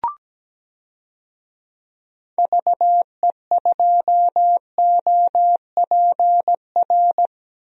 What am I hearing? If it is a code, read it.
Morse code, heard as a keyed tone, about 17 wpm, 700 Hz. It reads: VE2OPR
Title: DXCC-100 Call Signs - 6 Characters - 17wpm